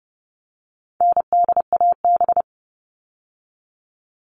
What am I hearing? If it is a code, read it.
Morse code, heard as a keyed tone, about 30 wpm, 700 Hz. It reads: NDAB
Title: All Cut Numbers - 4-Digit 30wpm